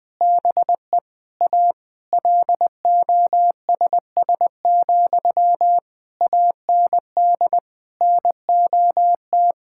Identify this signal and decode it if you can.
Morse code, heard as a keyed tone, about 20 wpm, 700 Hz. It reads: BE A LOSS, AND NOT
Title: Sentences from Top 1600 Words 20wpm